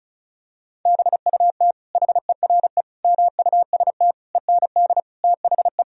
Morse code, heard as a keyed tone, about 34 wpm, 700 Hz. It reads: BUT HERE MUST END THE
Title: Sentences from Top 1000 Words - Review Part 1 - 35wpm